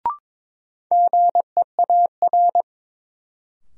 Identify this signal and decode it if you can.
Morse code, heard as a keyed tone, about 22 wpm, 700 Hz. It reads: GEAR
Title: Sentences from Top 1400 Words - New Word Drill Alt - 22wpm